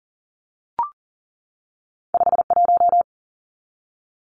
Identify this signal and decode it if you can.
Morse code, heard as a keyed tone, about 40 wpm, 700 Hz. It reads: 51
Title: Numbers - 2-Digit 40wpm